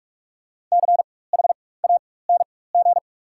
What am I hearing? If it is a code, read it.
Morse code, heard as a keyed tone, about 42 wpm, spaced as for 20 wpm, 690 Hz. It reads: CHANG
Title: 4x Character Spacing - Top 300 Words - 45wpm